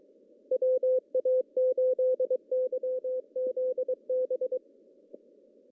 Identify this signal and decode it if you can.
Morse code, heard as a keyed tone, about 22 wpm, 510 Hz. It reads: WA8YZB
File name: CW / cqww-cw-k5tr